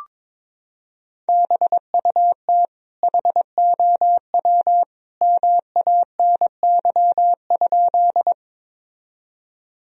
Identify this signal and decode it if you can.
Morse code, heard as a keyed tone, about 22 wpm, 700 Hz. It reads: BUT HOW MANY?
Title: Sentences from Top 1000 Words - Review Part 1 - 22wpm